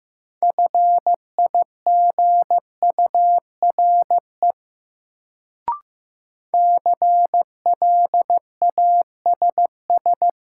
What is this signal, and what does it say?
Morse code, heard as a keyed tone, about 15 wpm, 700 Hz. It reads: FIGURE CLASS
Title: Sentences from Top 1000 Words - Review Part 2 - New Word Drill - 15wpm